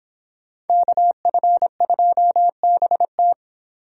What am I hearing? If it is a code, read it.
Morse code, heard as a keyed tone, about 24 wpm, 700 Hz. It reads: KF2BT